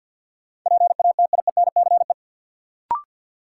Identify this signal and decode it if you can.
Morse code, heard as a keyed tone, about 49 wpm, 700 Hz. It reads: PATIENCE
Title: Sentences from Top 1300 Words - New Word Drill Alt - 50wpm